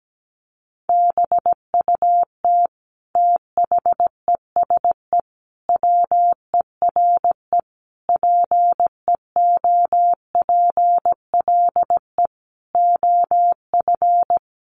Morse code, heard as a keyed tone, about 17 wpm, 700 Hz. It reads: BUT THESE WERE PEOPLE OF
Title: Sentences from Top 1500 Words 17wpm